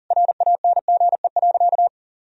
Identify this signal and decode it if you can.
Morse code, heard as a keyed tone, about 40 wpm, 700 Hz. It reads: RANGE.
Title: Sentences from Top 1000 Words - Review Part 4 - 40wpm